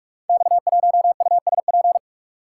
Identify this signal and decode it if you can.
Morse code, heard as a keyed tone, about 44 wpm, 700 Hz. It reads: X1USP
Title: Code Groups - 5 Letter:Numbers - 45wpm